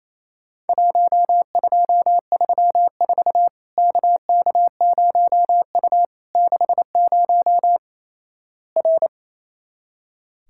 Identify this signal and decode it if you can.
Morse code, heard as a keyed tone, about 28 wpm, 700 Hz. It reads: 1234 KK0U 60 R
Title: Worked All Europe DX Contest - QTC Exchange - 28wpm